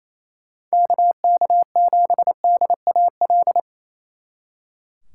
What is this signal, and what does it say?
Morse code, heard as a keyed tone, about 34 wpm, 700 Hz. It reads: KK7DAL
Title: Speed Racing - USA 2x3 Call Signs 28wpm